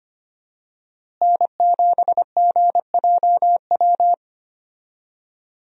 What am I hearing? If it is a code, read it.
Morse code, heard as a keyed tone, about 25 wpm, 700 Hz. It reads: N7GJW
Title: USA Call Signs - 5 Characters - Repeated 2x - 25wpm